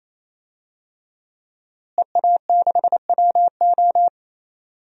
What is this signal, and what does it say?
Morse code, heard as a keyed tone, about 28 wpm, 700 Hz. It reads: EA6WO
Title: DXCC-100 Call Signs - 5 Characters - Repeated 3x - 28wpm